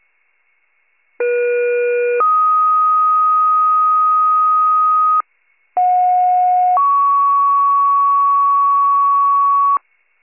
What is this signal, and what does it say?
Two-tone page, heard as a one-second tone, then a three-second tone, about 480 then 1250 Hz, about 730 then 1120 Hz. It is heard from 1.2 to 9.8 s.